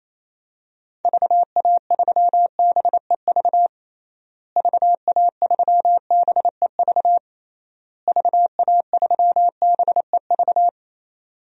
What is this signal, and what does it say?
Morse code, heard as a keyed tone, about 28 wpm, 700 Hz. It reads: VA3BEV VA3BEV VA3BEV
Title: DXCC-100 Call Signs - 6 Characters - Repeated 3x - 28wpm